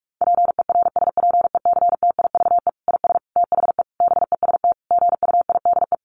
Morse code, heard as a keyed tone, about 44 wpm, 700 Hz. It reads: PERSPECTIVE IS THE BEST GUIDE
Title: Sentences from Top 1200 Words 45wpm